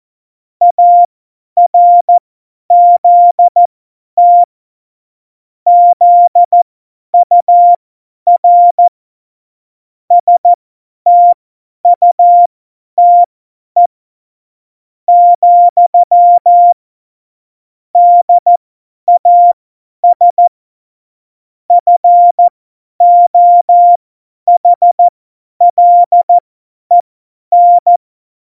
Morse code, heard as a keyed tone, about 14 wpm, spaced as for 10 wpm, 700 Hz. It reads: ARZT ZUR STUTE, DAS FOHLEN
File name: Morsekurs / Witze / Witz 28. August 2018, 50 F70